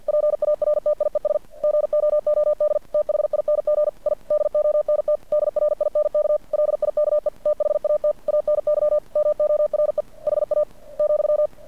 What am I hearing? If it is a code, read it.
Morse code, heard as a keyed tone, about 50 wpm, 600 Hz. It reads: PARTIES GOOD THING I DONT DRINK LIKE THAT ANY MORE HA ,
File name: archived / CRSnet